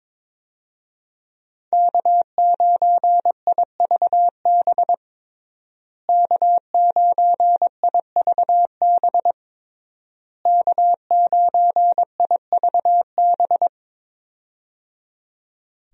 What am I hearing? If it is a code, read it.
Morse code, heard as a keyed tone, about 22 wpm, 700 Hz. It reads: K9IVB K9IVB K9IVB
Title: USA Call Signs - 5 Characters - Repeated 3x - 22wpm